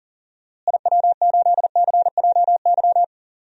Morse code, heard as a keyed tone, about 38 wpm, 700 Hz. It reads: IW8CJY
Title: DXCC-100 Call Signs - 6 Characters - Repeated 2x - 40wpm